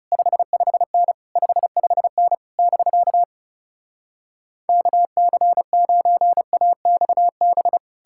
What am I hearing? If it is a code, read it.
Morse code, heard as a keyed tone, about 40 wpm, 700 Hz. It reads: 55N 55N <BK> KC9AXB